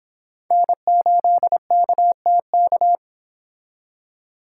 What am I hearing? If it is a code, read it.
Morse code, heard as a keyed tone, about 26 wpm, 700 Hz. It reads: N8KTK